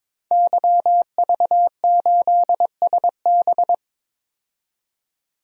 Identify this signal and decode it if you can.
Morse code, heard as a keyed tone, about 22 wpm, 700 Hz. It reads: YV8SB
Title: DXCC-100 Call Signs - 5 Characters - 22wpm